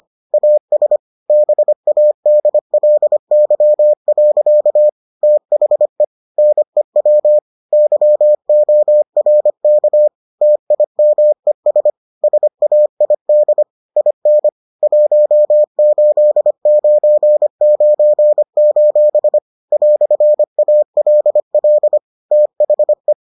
Morse code, heard as a keyed tone, about 25 wpm, 600 Hz. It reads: AS BADLY. THE NEW YORK TIMES SAID IN 1899: "ALL THE